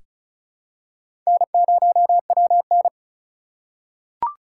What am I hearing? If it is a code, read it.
Morse code, heard as a keyed tone, about 36 wpm, 700 Hz. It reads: N0WN
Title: USA Call Signs - 4 Characters - 3x Character Spacing - 35wpm